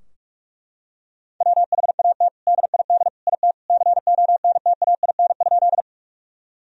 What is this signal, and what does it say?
Morse code, heard as a keyed tone, about 41 wpm, 700 Hz. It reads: WHAT DID IT CONTAIN?
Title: Sentences from Top 1000 Words - Review Part 2 - 45wpm